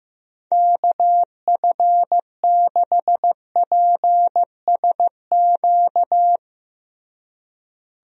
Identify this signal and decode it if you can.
Morse code, heard as a keyed tone, about 16 wpm, 700 Hz. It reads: KF6PSQ
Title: USA Call Signs - 6 Characters - Repeated 2x - 15wpm